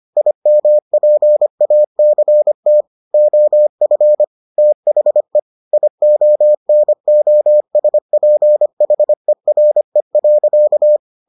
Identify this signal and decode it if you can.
Morse code, heard as a keyed tone, about 25 wpm, 600 Hz. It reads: IMPACT OF THE IONOSPHERE.